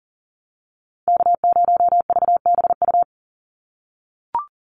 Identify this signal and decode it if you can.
Morse code, heard as a keyed tone, about 40 wpm, 700 Hz. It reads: K0VBU
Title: Call Signs - 40wpm